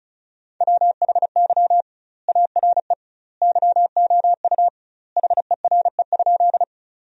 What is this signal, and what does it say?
Morse code, heard as a keyed tone, about 36 wpm, 700 Hz. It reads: WHY ARE YOU HERE?
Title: 2x Word-Spacing - Sentences from Top 200 Words - 35wpm